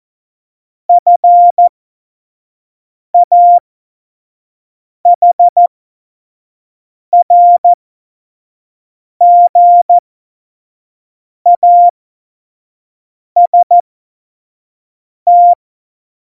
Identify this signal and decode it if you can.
Morse code, heard as a keyed tone, about 14 wpm, spaced as for 5 wpm, 700 Hz. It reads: FAHRGAST
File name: Morsekurs / Witze / Witz 10. Oktober 17, Tempo 25